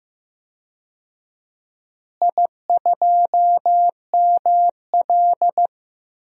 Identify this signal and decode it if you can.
Morse code, heard as a keyed tone, about 16 wpm, 700 Hz. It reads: I2ML
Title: DXCC-100 Call Signs - 4 Characters - Repeated 2x - 15wpm